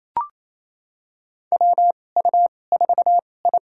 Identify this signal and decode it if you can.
Morse code, heard as a keyed tone, about 28 wpm, spaced as for 20 wpm, 700 Hz. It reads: WU4I
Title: USA Call Signs - 2x Character Spacing - 28wpm